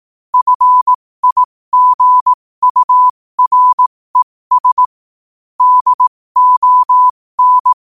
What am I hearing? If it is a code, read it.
Morse code, heard as a keyed tone, about 18 wpm, spaced as for 15 wpm, 1000 Hz. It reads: FIGURES DON
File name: qotdfiles / 15wpm